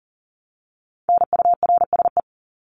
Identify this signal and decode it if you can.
Morse code, heard as a keyed tone, about 40 wpm, 700 Hz. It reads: NURSE
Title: Top 1751-2000 Words 40wpm